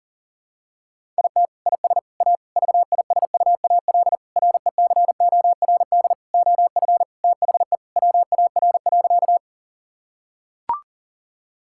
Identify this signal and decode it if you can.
Morse code, heard as a keyed tone, about 37 wpm, 700 Hz. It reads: IT IS A VISUAL RECORD OF THE WAR.
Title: Sentences from Top 1000 Words - Review Part 6 - 40wpm